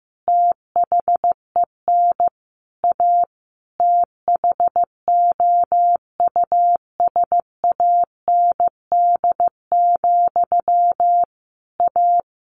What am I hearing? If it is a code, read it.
Morse code, heard as a keyed tone, about 15 wpm, 700 Hz. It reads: THEN A THOUSAND, A
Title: A Sound Of Thunder 15wpm